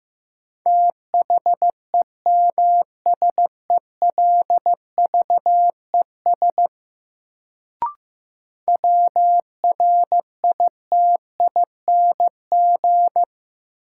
Morse code, heard as a keyed tone, about 15 wpm, 700 Hz. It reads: THEMSELVES WRITING
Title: Sentences from Top 800 Words - New Word Drill - 15wpm